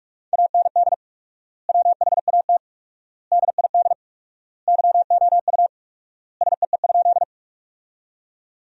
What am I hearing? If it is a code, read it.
Morse code, heard as a keyed tone, about 46 wpm, 700 Hz. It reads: AND WHAT DID YOU SEE?
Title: 4x Word-Spacing - Sentences from Top 100 Words - 45wpm